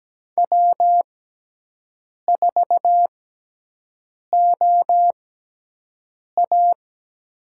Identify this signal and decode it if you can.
Morse code, heard as a keyed tone, about 17 wpm, spaced as for 6 wpm, 700 Hz. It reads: W4OA
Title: USA Call Signs - 6x Character Spacing - 17wpm